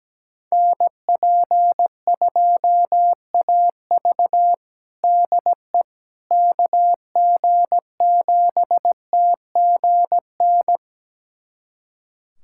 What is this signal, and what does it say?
Morse code, heard as a keyed tone, about 17 wpm, 700 Hz. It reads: NP2AV DE KG7TGN
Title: USA Callsign DE Callsign 17wpm